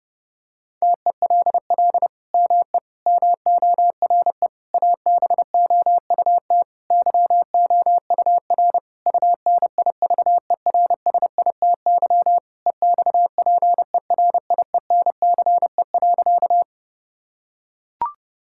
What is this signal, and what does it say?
Morse code, heard as a keyed tone, about 32 wpm, 700 Hz. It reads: TELL ME MORE ABOUT YOUR UNIVERSITY EXPERIENCE.
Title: Sentences from Top 1000 Words - Review Part 5 - 30wpm